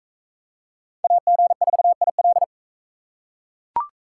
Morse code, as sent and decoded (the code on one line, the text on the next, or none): .- --. ....- .. .-..
AG4IL